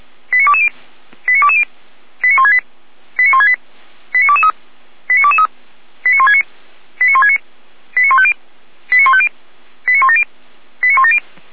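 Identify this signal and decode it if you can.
Five-tone selective call (ZVEI), heard as quick runs of five tones, tones of about 70 ms. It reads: 89220 89220 89168 89168 89223 89223 89169 89169 89150 89150 89170 89170
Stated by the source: ZVEI five-tone call 89167 received by SDR, from audio server of Feuerwehr Orlamünde (fire brigade).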